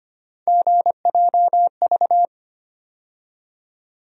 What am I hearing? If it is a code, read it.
Morse code, heard as a keyed tone, about 25 wpm, 700 Hz. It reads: GJV
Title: USA Call Sign Postfixes - 3 Letters 25wpm